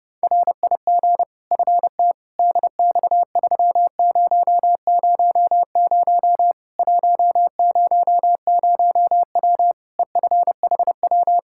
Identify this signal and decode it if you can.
Morse code, heard as a keyed tone, about 32 wpm, 700 Hz. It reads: RIG FT DX3000 100W EFHW
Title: K1USN SST QSO Practice 30wpm